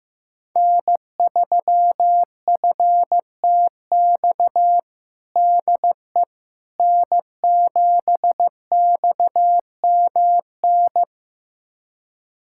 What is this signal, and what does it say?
Morse code, heard as a keyed tone, about 15 wpm, 700 Hz. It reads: N3FTX DE N7XMN
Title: USA Callsign DE Callsign - 5 Characters - 15wpm